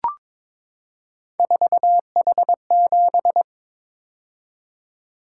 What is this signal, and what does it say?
Morse code, heard as a keyed tone, about 23 wpm, 700 Hz. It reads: 4H7
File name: practice / 22wpm / Three Letter-Number Combinations - 22wpm